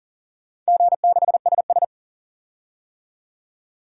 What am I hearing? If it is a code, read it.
Morse code, heard as a keyed tone, about 40 wpm, 700 Hz. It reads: G6SS
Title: DXCC-100 Call Signs - 4 Characters - 40wpm